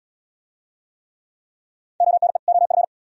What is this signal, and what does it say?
Morse code, heard as a keyed tone, about 76 wpm, 700 Hz. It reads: ONE OF